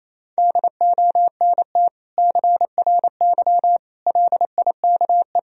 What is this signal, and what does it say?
Morse code, heard as a keyed tone, about 27 wpm, 700 Hz. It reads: DONT CRY LIKE
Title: Sentences from Top 1000 Words - Review Part 2 - 28wpm